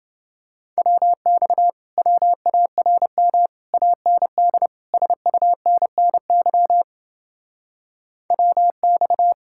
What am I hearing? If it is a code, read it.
Morse code, heard as a keyed tone, about 31 wpm, 700 Hz. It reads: WX WARM AND SUNNY WX